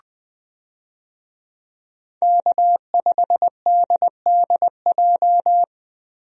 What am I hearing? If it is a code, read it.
Morse code, heard as a keyed tone, about 20 wpm, 700 Hz. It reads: K5DDJ
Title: USA Call Signs - 5 Characters - Repeated 2x - 20wpm